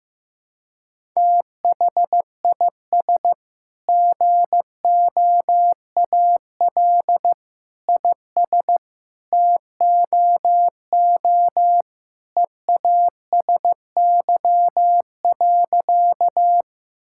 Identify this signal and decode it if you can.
Morse code, heard as a keyed tone, about 15 wpm, 700 Hz. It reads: THIS GOAL IS TOO EASY.
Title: Sentences from Top 1000 Words - Review Part 5 - 15wpm